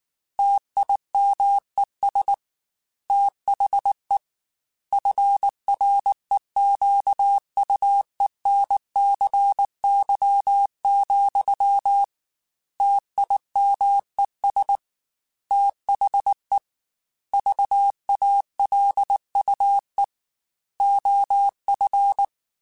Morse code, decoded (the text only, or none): TIMES THE FREQUENCY, TIMES THE VALUE OF